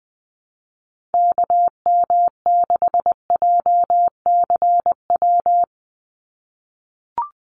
Morse code, heard as a keyed tone, about 19 wpm, 700 Hz. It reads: KM6JCW
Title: USA Call Signs 20wpm